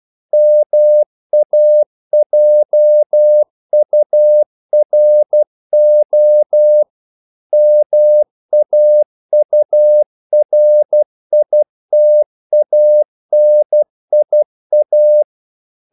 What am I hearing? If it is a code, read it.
Morse code, heard as a keyed tone, about 12 wpm, 600 Hz. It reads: MAJURO MAURITANIA
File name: CW / Practice / NationalCapitals12